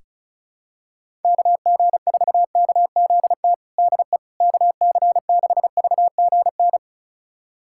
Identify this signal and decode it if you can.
Morse code, heard as a keyed tone, about 34 wpm, 700 Hz. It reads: KG4KZT DE KC6VGN
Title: USA Callsign DE Callsign 35wpm